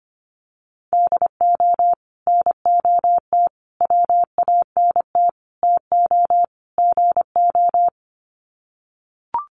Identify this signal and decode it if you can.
Morse code, heard as a keyed tone, about 25 wpm, 700 Hz. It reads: DO NOT WANT TO GO
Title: Sets of 5 Words 25wpm